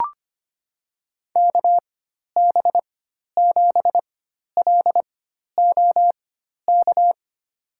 Morse code, heard as a keyed tone, about 25 wpm, spaced as for 12 wpm, 700 Hz. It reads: KB7LOK
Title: USA Call Signs - 6 Characters - 4x Character Spacing - 25wpm